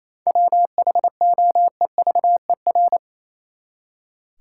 Morse code, heard as a keyed tone, about 28 wpm, 700 Hz. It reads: WHOEVER
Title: Seven Letter Words - 28wpm